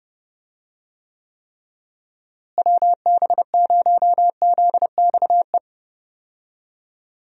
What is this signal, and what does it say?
Morse code, heard as a keyed tone, about 31 wpm, 700 Hz. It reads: WB0ZXE
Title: USA Call Signs - 6 Characters - Repeated 3x - 30wpm